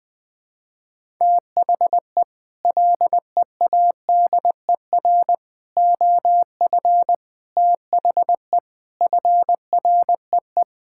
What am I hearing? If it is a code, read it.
Morse code, heard as a keyed tone, about 20 wpm, 700 Hz. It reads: THE LEADER OF THE FREE
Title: Sets of 6 Words - 20wpm